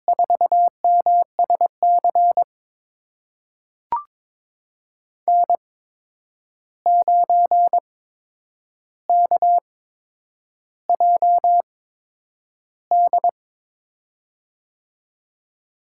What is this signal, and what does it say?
Morse code, heard as a keyed tone, about 22 wpm, spaced as for 9 wpm, 700 Hz. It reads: 4MSC N9KJD
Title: USA Call Signs - 5 Characters - 8x Character Spacing - 22wpm